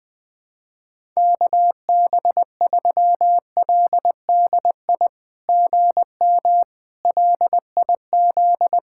Morse code, heard as a keyed tone, about 20 wpm, 700 Hz. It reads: KB3LDI GM LIZ
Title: SOTA QSO Practice 20wpm